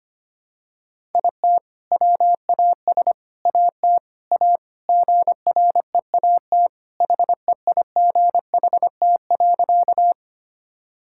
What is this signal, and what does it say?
Morse code, heard as a keyed tone, about 25 wpm, 700 Hz. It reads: IT WAS AT A GREAT HEIGHT.
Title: Sentences from Top 1600 Words 25wpm